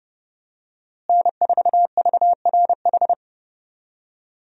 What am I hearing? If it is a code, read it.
Morse code, heard as a keyed tone, about 32 wpm, 700 Hz. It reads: N4VRH
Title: USA Call Signs - 5 Characters - Repeated 2x - 30wpm